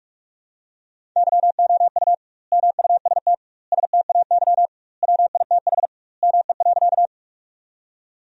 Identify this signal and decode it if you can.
Morse code, heard as a keyed tone, about 41 wpm, 700 Hz. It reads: YOU MUST STAY WITH ME.
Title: 2x Word-Spacing - Sentences from Top 500 Words - 45wpm